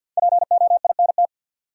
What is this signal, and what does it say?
Morse code, heard as a keyed tone, about 48 wpm, 700 Hz. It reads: POINT
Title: Sentences from Top 200 Words - New Word Drill - 50wpm